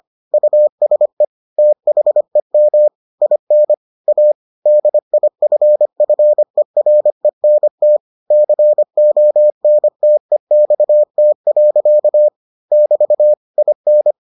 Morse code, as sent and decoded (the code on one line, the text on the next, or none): ..- ... / - .... -- / .. -. / .- / -.. .. ..-. ..-. .-. -. - / -.-. --- -. - . -..- - .-.-.- / -...- / .. -.
USE THEM IN A DIFFERENT CONTEXT. <BT> IN